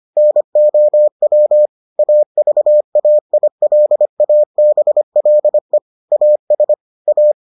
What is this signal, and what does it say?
Morse code, heard as a keyed tone, about 25 wpm, 600 Hz. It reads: NOW AVAILABLE AS A